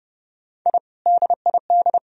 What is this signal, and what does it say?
Morse code, heard as a keyed tone, about 34 wpm, 700 Hz. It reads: I DID